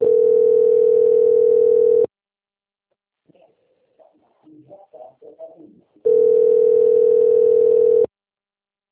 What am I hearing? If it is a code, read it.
Ringback tone (North America), 440 + 480 Hz.